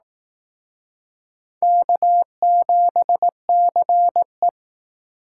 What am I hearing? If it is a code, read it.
Morse code, heard as a keyed tone, about 18 wpm, 700 Hz. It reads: K7CE
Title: Speed Racing - USA 1x2 Call Signs 15wpm